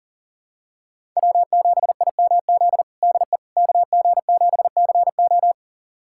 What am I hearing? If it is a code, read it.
Morse code, heard as a keyed tone, about 40 wpm, 700 Hz. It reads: W7IMZ DE KG7CO
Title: USA Callsign DE Callsign - 5 Characters - 40wpm